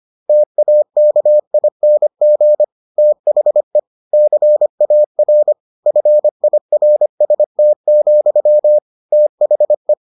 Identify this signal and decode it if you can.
Morse code, heard as a keyed tone, about 25 wpm, 600 Hz. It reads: TAKING THE CAR FIRST, THE